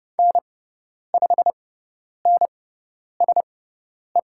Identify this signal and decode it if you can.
Morse code, heard as a keyed tone, about 30 wpm, spaced as for 10 wpm, 700 Hz. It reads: N5NSE
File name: Lesson 17g - Letter u - Call Signs 30wpm